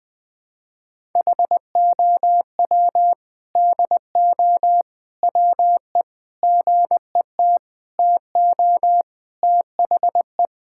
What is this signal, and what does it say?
Morse code, heard as a keyed tone, about 20 wpm, 700 Hz. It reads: HOW DO WE GET TO THE
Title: Sentences with 4-Word Phrases - Spoken Beginning - 20wpm